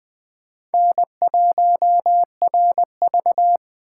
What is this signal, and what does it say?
Morse code, heard as a keyed tone, about 20 wpm, 700 Hz. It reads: N1RV